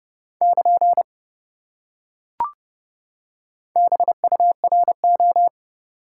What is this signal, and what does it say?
Morse code, heard as a keyed tone, about 27 wpm, 700 Hz. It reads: <KN> BURO